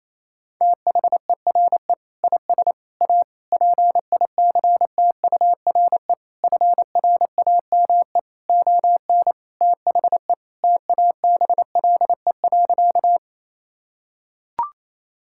Morse code, heard as a keyed tone, about 28 wpm, 700 Hz. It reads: THERE IS A PICTURE FRAME ON THE TABLE.
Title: Sentences from Top 1000 Words - Review Part 6 - 28wpm